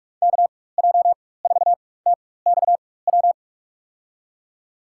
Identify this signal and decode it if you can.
Morse code, heard as a keyed tone, about 41 wpm, spaced as for 20 wpm, 700 Hz. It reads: KJ4TXW
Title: USA Call Signs - 6 Characters - 4x Character Spacing - 45wpm